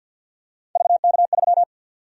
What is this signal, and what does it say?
Morse code, heard as a keyed tone, about 51 wpm, 700 Hz. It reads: VK3
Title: USA Call Sign Prefixes - 50wpm